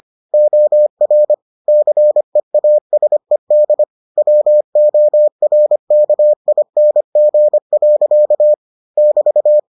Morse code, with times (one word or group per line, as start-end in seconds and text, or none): 0.3-1.3 OR
1.7-3.8 CEASED
4.2-8.5 WORKING.
9.0-9.6 <BT>